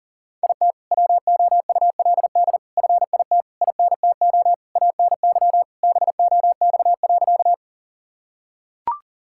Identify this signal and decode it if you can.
Morse code, heard as a keyed tone, about 37 wpm, 700 Hz. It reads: IT WOULD FIT INTO ANY BOX.
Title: Sentences from Top 1000 Words - Review Part 5 - 40wpm